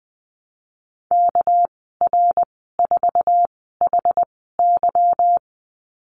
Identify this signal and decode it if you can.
Morse code, heard as a keyed tone, about 20 wpm, 700 Hz. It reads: K R 4 H Y
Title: 231 Lesson 37g - Letter x - Call Signs 20wpm